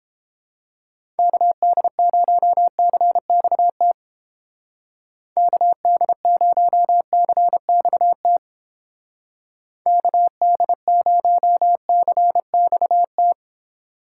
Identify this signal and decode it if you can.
Morse code, heard as a keyed tone, about 30 wpm, 700 Hz. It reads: KD0CXT KD0CXT KD0CXT